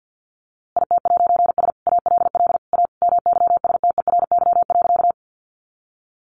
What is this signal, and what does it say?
Morse code, heard as a keyed tone, about 50 wpm, 700 Hz. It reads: IT'S ALL A MYSTERY.
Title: A Sound Of Thunder 50wpm